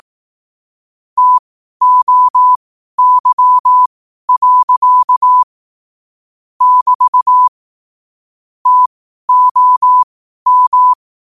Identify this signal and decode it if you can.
Morse code, heard as a keyed tone, about 18 wpm, spaced as for 12 wpm, 1000 Hz. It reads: TOY. <BT> TOM
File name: qotdfiles / 12wpm